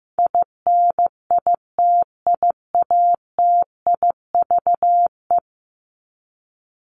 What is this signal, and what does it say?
Morse code, heard as a keyed tone, about 15 wpm, 700 Hz. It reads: INITIATIVE
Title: Top 1751-2000 Words 15wpm